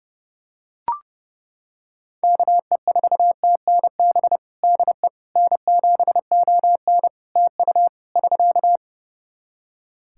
Morse code, heard as a keyed tone, about 32 wpm, 700 Hz. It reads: KE4TNB DE N7ON TU <SK>
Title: Practice QSO - Exchange 4 - Ending - 30wpm